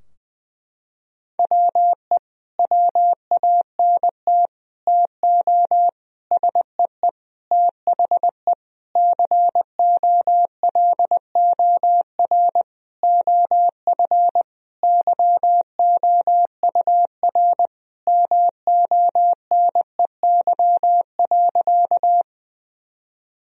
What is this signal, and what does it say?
Morse code, heard as a keyed tone, about 20 wpm, 700 Hz. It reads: WE WANT TO SEE THE COLOR OF YOUR MONEY.
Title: Sentences from Top 1000 Words - Review Part 2 - 20wpm